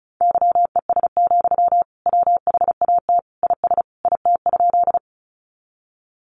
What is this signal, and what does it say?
Morse code, heard as a keyed tone, about 34 wpm, 700 Hz. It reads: YES, WHAT IS IT?
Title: Sentences from Top 500 Words 35wpm